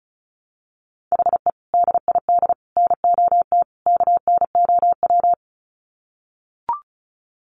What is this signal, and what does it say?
Morse code, heard as a keyed tone, about 34 wpm, 700 Hz. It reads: HE DID NOT KNOW
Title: Sets of 4 Words 35wpm